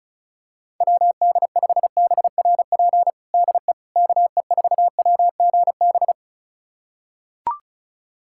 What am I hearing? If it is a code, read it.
Morse code, heard as a keyed tone, about 34 wpm, 700 Hz. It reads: WD5BRP DE KE4WGB
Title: USA Callsign DE Callsign 35wpm